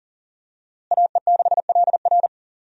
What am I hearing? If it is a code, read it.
Morse code, heard as a keyed tone, about 40 wpm, 700 Hz. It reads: AE6LR
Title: USA Call Signs - 5 Characters - Repeated 2x - 40wpm